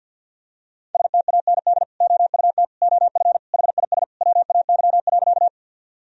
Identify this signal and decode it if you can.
Morse code, heard as a keyed tone, about 52 wpm, 700 Hz. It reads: STAND OUT OF HIS WAY.
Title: Sentences from Top 1000 Words - Review Part 1 - 50wpm